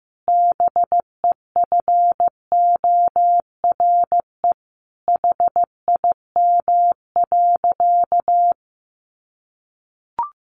Morse code, heard as a keyed tone, about 15 wpm, 700 Hz. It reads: BEFORE HIM.
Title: Sentences from Top 1400 Words 15wpm